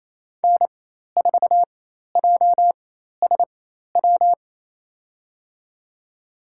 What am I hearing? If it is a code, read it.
Morse code, heard as a keyed tone, about 28 wpm, spaced as for 13 wpm, 700 Hz. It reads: N4JSW
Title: USA Call Signs - 5 Characters - 4x Character Spacing - 28wpm